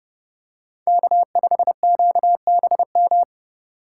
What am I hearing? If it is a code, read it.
Morse code, heard as a keyed tone, about 32 wpm, 700 Hz. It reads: K5QBM
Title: USA Call Signs - 5 Characters - 8x Character Spacing - 30wpm